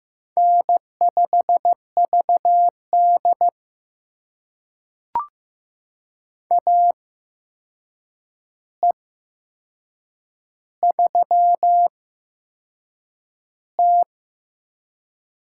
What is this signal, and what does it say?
Morse code, heard as a keyed tone, about 15 wpm, spaced as for 6 wpm, 700 Hz. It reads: N5VD AE3T
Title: USA Call Signs - 4 Characters - 8x Character Spacing - 15wpm